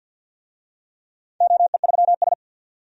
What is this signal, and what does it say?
Morse code, heard as a keyed tone, about 53 wpm, 700 Hz. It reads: OE3S
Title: DXCC-100 Call Signs - 4 Characters - 50wpm